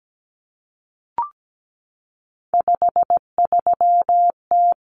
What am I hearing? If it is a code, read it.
Morse code, heard as a keyed tone, about 17 wpm, 690 Hz. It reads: 53T
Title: Common Cut Numbers - 3-Digit 17wpm